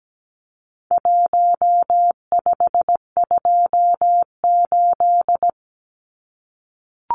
Morse code, heard as a keyed tone, about 17 wpm, 700 Hz. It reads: 1528
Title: Numbers - Contest Style - 4-Digit 17wpm